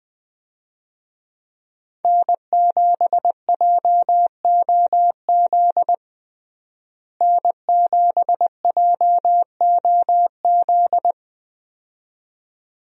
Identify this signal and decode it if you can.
Morse code, heard as a keyed tone, about 20 wpm, 700 Hz. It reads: N7JOZ N7JOZ
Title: USA Call Signs - 5 Characters - Repeated 2x - 20wpm